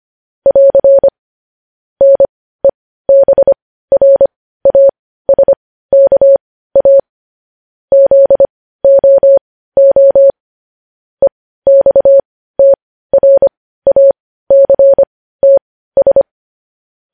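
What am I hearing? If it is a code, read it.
Morse code, heard as a keyed tone, about 25 wpm, spaced as for 15 wpm, 550 Hz. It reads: <AR> NEBRASKA ZOO EXTRACTS
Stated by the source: Morse code news headlines at 15 wpm.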